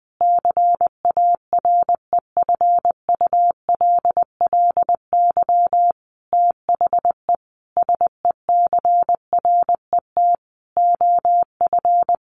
Morse code, heard as a keyed tone, about 20 wpm, 700 Hz. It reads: CAREFULLY THE SECRET OF
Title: Sentences from Top 1300 Words 20wpm